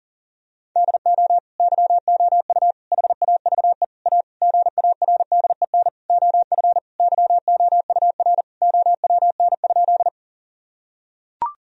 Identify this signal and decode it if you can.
Morse code, heard as a keyed tone, about 40 wpm, 700 Hz. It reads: DO YOU HAVE A GARDEN OF YOUR OWN?
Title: Sentences from Top 1000 Words - Review Part 4 - 40wpm